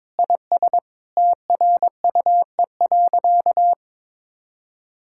Morse code, heard as a keyed tone, about 22 wpm, 700 Hz. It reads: IS TRUE.
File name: practice / 22wpm / Sentences with 4-Word Phrases - 22wpm